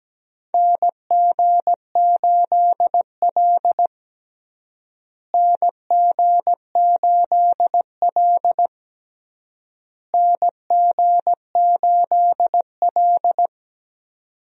Morse code, heard as a keyed tone, about 17 wpm, 700 Hz. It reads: NG8L NG8L NG8L
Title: USA Call Signs - 4 Characters - Repeated 3x - 17wpm